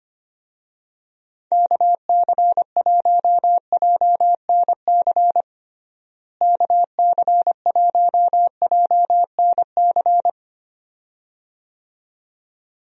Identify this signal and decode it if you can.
Morse code, heard as a keyed tone, about 25 wpm, 700 Hz. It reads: KC1JNC KC1JNC
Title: USA Call Signs - 6 Characters - Repeated 2x - 25wpm